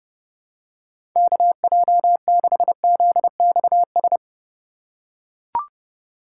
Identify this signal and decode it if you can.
Morse code, heard as a keyed tone, about 32 wpm, 700 Hz. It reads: KJ6ZXS
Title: USA Call Signs - 2x Character Spacing - 30wpm